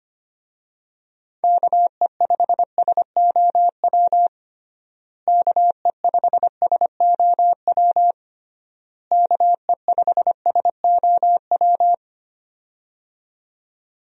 Morse code, heard as a keyed tone, about 25 wpm, 700 Hz. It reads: KE5SOW KE5SOW KE5SOW
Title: USA Call Signs - 6 Characters - Repeated 3x - 25wpm